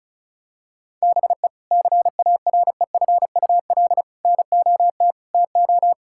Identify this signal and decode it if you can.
Morse code, heard as a keyed tone, about 34 wpm, 700 Hz. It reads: BE CAREFUL NOT TO
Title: Sentences from Top 1600 Words 35wpm